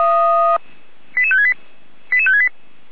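Five-tone selective call (ZVEI), heard as quick runs of five tones, tones of about 70 ms. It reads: 88568 88568
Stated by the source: ZVEI five-tone selective call 88355 received on SDR.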